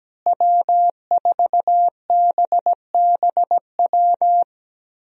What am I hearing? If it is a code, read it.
Morse code, heard as a keyed tone, about 17 wpm, 700 Hz. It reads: W4BBW
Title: USA Call Signs - 5 Characters - Repeated 3x - 17wpm